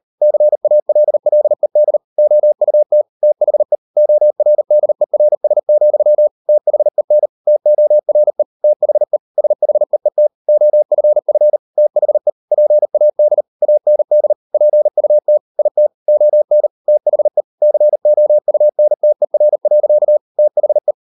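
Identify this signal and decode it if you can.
Morse code, heard as a keyed tone, about 38 wpm, 600 Hz. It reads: CALLED OUT THE ORDERS, THEN TORE THE SHEET OFF THE PAD AND PUT IT ON THE COUNTER. THE
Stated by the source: Morse code practice at 38 WPM, sending Ken Follett's novel Never in English, from CW med Gnister.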